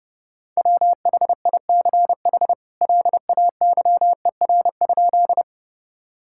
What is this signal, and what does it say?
Morse code, heard as a keyed tone, about 30 wpm, 700 Hz. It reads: WHICH LAYER?
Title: Sentences from Top 1600 Words 30wpm